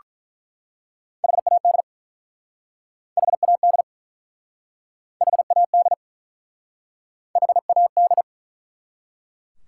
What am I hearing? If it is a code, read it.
Morse code, heard as a keyed tone, about 48 wpm, 700 Hz. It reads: HAD HAD HAD H A D